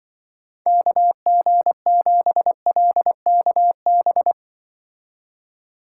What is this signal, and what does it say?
Morse code, heard as a keyed tone, about 25 wpm, 700 Hz. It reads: KG7LKB
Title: Speed Racing - USA 2x3 Call Signs 20wpm